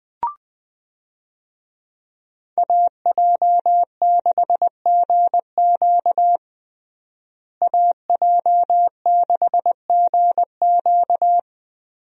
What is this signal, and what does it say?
Morse code, heard as a keyed tone, about 21 wpm, 700 Hz. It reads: AJ6GQ AJ6GQ
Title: USA Call Signs - 5 Characters - Repeated 3x - 20wpm